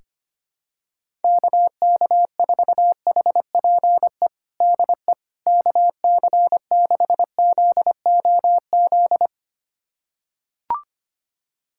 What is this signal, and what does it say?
Morse code, heard as a keyed tone, about 25 wpm, 700 Hz. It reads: KK4HPE DE KC6ZOZ
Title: USA Callsign DE Callsign - 6 Characters - 25wpm